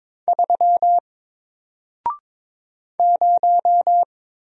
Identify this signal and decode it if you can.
Morse code, heard as a keyed tone, about 22 wpm, 700 Hz. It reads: 3 0
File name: Single-Letter-Number-22wpm